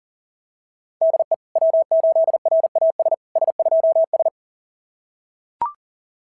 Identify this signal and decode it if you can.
Morse code, heard as a keyed tone, about 37 wpm, 690 Hz. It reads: DE W8RAS S2S
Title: SOTA QSO Practice 40wpm